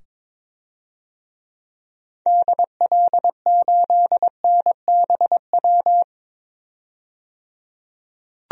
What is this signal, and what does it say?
Morse code, heard as a keyed tone, about 21 wpm, 700 Hz. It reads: DL8NBW
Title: DXCC-100 Call Signs - 6 Characters - 22wpm